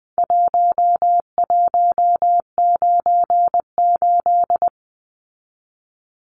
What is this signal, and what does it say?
Morse code, heard as a keyed tone, about 20 wpm, 700 Hz. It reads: 1198
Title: Numbers - Contest Style - 4-Digit 20wpm